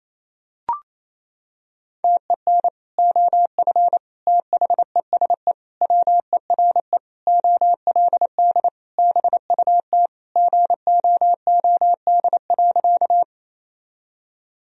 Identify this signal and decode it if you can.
Morse code, heard as a keyed tone, about 28 wpm, 700 Hz. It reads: TEN OF THESE WERE OLD BUT GOOD.
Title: Sentences from Top 1000 Words - Review Part 2 - 28wpm